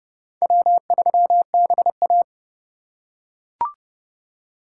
Morse code, heard as a keyed tone, about 29 wpm, 700 Hz. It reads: W3BA